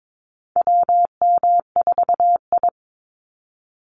Morse code, heard as a keyed tone, about 22 wpm, 700 Hz. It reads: WM4I
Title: Call Signs Repeated 3x - 22wpm